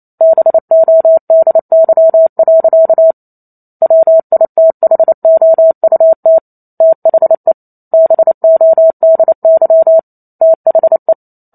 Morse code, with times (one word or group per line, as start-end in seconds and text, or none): 0.2-3.1 BODY.
3.8-6.4 WITHOUT
6.8-7.5 THE
7.9-10.0 BODY
10.4-11.1 THE